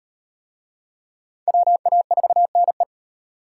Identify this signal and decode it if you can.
Morse code, heard as a keyed tone, about 36 wpm, 700 Hz. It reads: WA4NE